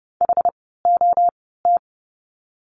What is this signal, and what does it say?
Morse code, heard as a keyed tone, about 30 wpm, spaced as for 17 wpm, 700 Hz. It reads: HOT
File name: Lesson 16f - Letter c - Words 30wpm